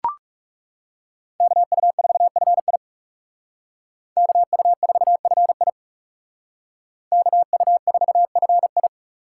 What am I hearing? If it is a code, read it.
Morse code, heard as a keyed tone, about 38 wpm, 700 Hz. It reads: KU4FI KU4FI KU4FI